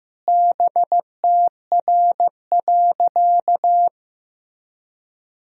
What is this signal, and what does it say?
Morse code, heard as a keyed tone, about 15 wpm, 700 Hz. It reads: BTR.
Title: Sentences from Top 400 Words and Abbreviations 15wpm